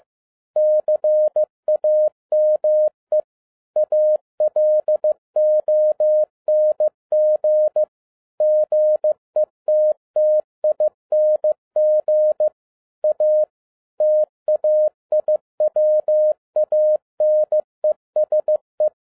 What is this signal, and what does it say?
Morse code, heard as a keyed tone, about 15 wpm, 610 Hz. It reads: CAME ALONG GETTING A TAIWANESE
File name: 2011-07-mp3 / n07 A LOT HAS CHANGED 15wpm